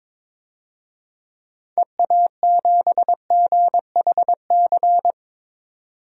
Morse code, heard as a keyed tone, about 22 wpm, 700 Hz. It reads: EA7GHC
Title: DXCC-100 Call Signs - 6 Characters - 22wpm